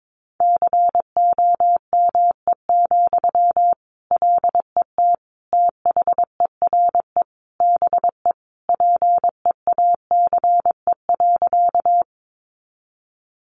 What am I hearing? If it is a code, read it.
Morse code, heard as a keyed tone, about 22 wpm, 700 Hz. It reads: COME, LET THERE BE PEACE.
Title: Sentences from Top 1500 Words 22wpm